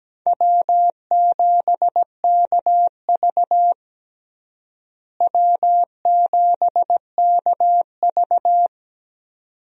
Morse code, heard as a keyed tone, about 17 wpm, 700 Hz. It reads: W7KV W7KV
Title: USA Call Signs - 4 Characters - Repeated 2x - 17wpm